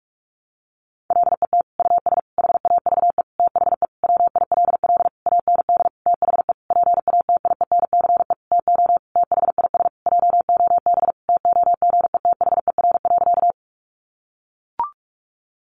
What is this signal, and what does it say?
Morse code, heard as a keyed tone, about 44 wpm, 700 Hz. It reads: LET US HAVE THE WILL AND THE PATIENCE TO THIS JOB TOGETHER.
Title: Sentences from Top 1300 Words 45wpm